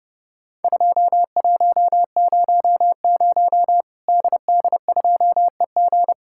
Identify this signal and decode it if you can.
Morse code, heard as a keyed tone, about 32 wpm, 700 Hz. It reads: 2100 DD2EG
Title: Worked All Europe DX Contest - QTC Exchange - 30wpm